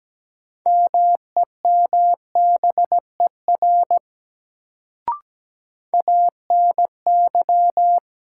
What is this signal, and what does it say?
Morse code, heard as a keyed tone, about 17 wpm, 700 Hz. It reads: MEMBER ANY
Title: Sets of 3 Words - New Word Drill - 17wpm